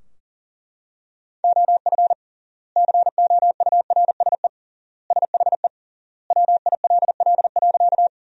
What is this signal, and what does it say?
Morse code, heard as a keyed tone, about 40 wpm, 700 Hz. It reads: OF COURSE SHE WILL.
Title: 3x Word-Spacing - Sentences from Top 500 Words - 40wpm